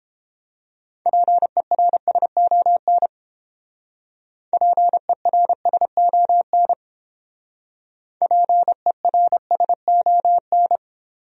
Morse code, heard as a keyed tone, about 30 wpm, 700 Hz. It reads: PERSON PERSON PERSON